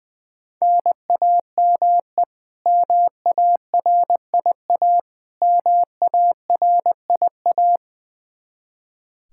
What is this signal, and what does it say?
Morse code, heard as a keyed tone, about 19 wpm, 700 Hz. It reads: NAME MARIA MARIA
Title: Practice QSO - Exchange 1 - Beginning - 20wpm